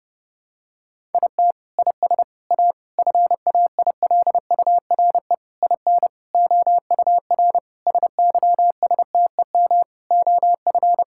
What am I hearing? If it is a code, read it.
Morse code, heard as a keyed tone, about 30 wpm, 700 Hz. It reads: IT IS A FAILURE IN OUR SYSTEM OF
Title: Sentences from Top 1000 Words - Review Part 6 - 30wpm